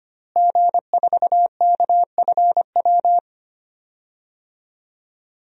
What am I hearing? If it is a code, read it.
Morse code, heard as a keyed tone, about 25 wpm, 700 Hz. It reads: G4KFW
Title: DXCC-100 Call Signs - 5 Characters - Repeated 2x - 25wpm